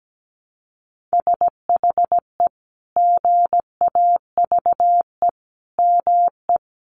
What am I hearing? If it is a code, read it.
Morse code, heard as a keyed tone, about 17 wpm, 700 Hz. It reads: SHE GAVE ME
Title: Sentences from Top 500 Words 17wpm